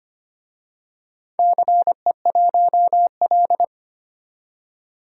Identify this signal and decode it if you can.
Morse code, heard as a keyed tone, about 25 wpm, 700 Hz. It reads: CE1L
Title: DXCC-100 Call Signs - 4 Characters - 25wpm